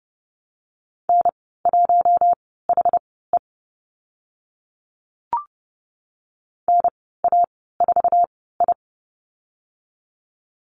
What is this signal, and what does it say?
Morse code, heard as a keyed tone, about 30 wpm, spaced as for 17 wpm, 700 Hz. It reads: N1HE NA4I
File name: Lesson 15g - Number 5 - Call Signs 30wpm